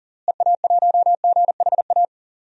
Morse code, heard as a keyed tone, about 40 wpm, 700 Hz. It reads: EA1GHA
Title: DXCC-100 Call Signs - 6 Characters - Repeated 3x - 40wpm